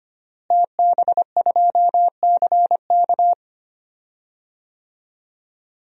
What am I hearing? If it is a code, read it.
Morse code, heard as a keyed tone, about 25 wpm, 700 Hz. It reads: TB2CK
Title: DXCC-100 Call Signs - 5 Characters - Repeated 2x - 25wpm